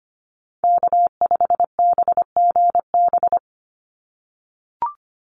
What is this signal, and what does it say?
Morse code, heard as a keyed tone, about 25 wpm, 700 Hz. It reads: K5BGB
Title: Call Signs - 25wpm